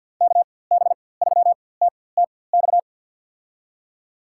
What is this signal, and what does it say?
Morse code, heard as a keyed tone, about 52 wpm, spaced as for 23 wpm, 700 Hz. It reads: KB3TTX
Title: USA Call Signs - 6 Characters - 4x Character Spacing - 50wpm